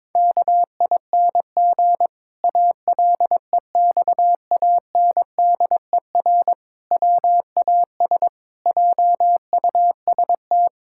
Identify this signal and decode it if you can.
Morse code, heard as a keyed tone, about 22 wpm, 700 Hz. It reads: KING ALEXANDER WAS JUST
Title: Pangram Challenge v2 - 22wpm